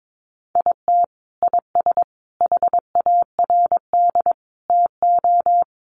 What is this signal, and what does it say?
Morse code, heard as a keyed tone, about 22 wpm, 700 Hz. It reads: IT IS HARD TO
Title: Sets of 5 Words Encore2 22wpm